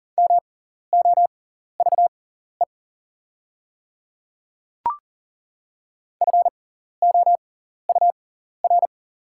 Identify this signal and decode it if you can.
Morse code, heard as a keyed tone, about 40 wpm, spaced as for 14 wpm, 690 Hz. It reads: MOVE FOUR
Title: 6x Character Spacing - Top 300 Words - 40wpm